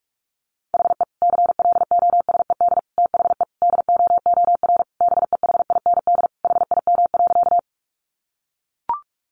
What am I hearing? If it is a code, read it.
Morse code, heard as a keyed tone, about 44 wpm, 700 Hz. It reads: HE CLOSED THE DOOR BEHIND HIM.
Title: Sentences from Top 1400 Words 45wpm